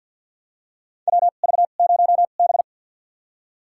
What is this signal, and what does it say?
Morse code, heard as a keyed tone, about 48 wpm, spaced as for 35 wpm, 700 Hz. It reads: WV0B
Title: USA Call Signs - 4 Characters - 2x Character Spacing - 50wpm